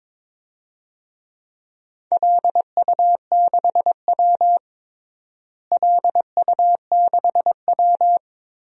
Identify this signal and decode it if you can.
Morse code, heard as a keyed tone, about 22 wpm, 700 Hz. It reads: LU6W LU6W
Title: DXCC-100 Call Signs - 4 Characters - Repeated 3x - 22wpm